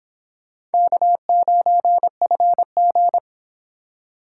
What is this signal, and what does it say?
Morse code, heard as a keyed tone, about 25 wpm, 700 Hz. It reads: K9FG